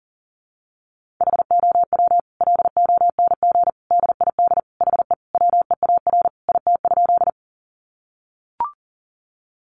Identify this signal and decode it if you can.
Morse code, heard as a keyed tone, about 40 wpm, 700 Hz. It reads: HOW LONG DID HE WEAR IT?
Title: Sentences from Top 1500 Words 40wpm